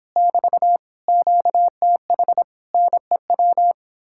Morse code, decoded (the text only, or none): <BT> QTH NEW